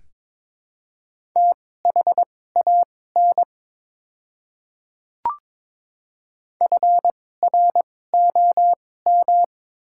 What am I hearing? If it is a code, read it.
Morse code, heard as a keyed tone, about 22 wpm, spaced as for 16 wpm, 700 Hz. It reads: THAN FROM
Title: 2x Character Spacing - Top 100 Words - 22wpm